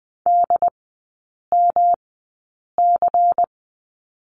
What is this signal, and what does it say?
Morse code, heard as a keyed tone, about 20 wpm, spaced as for 8 wpm, 700 Hz. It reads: DMC
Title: 152 Lesson 25e - Letter p - Three Characters 20wpm